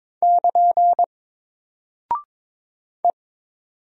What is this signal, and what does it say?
Morse code, heard as a keyed tone, about 22 wpm, 700 Hz. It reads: <KN> E
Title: Single Character (Letter:Number:Punctuation:Prosign) Alt 22wpm